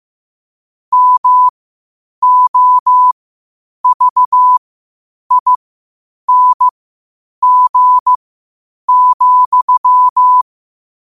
Morse code, heard as a keyed tone, about 15 wpm, spaced as for 8 wpm, 1000 Hz. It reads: MOVING,
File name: qotdfiles / 8wpm